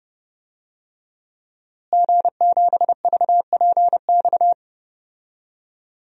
Morse code, heard as a keyed tone, about 30 wpm, 700 Hz. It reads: G7VPX
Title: DXCC-100 Call Signs - 5 Characters - Repeated 3x - 30wpm